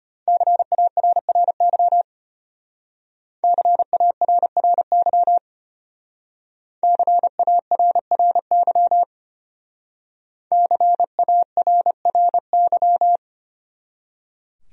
Morse code, heard as a keyed tone, about 31 wpm, 700 Hz. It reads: CARRY CARRY CARRY CARRY
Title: Speed-Racing Top 300 Words 25wpm